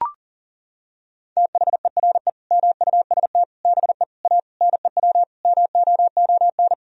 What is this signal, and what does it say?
Morse code, heard as a keyed tone, about 40 wpm, 700 Hz. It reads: THERE MUST BE A NEW MOON
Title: Sentences from Top 1000 Words - Review Part 3 - 40wpm